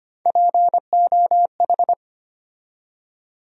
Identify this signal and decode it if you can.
Morse code, heard as a keyed tone, about 25 wpm, 700 Hz. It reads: POH
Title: USA Call Sign Postfixes - 3 Letters 25wpm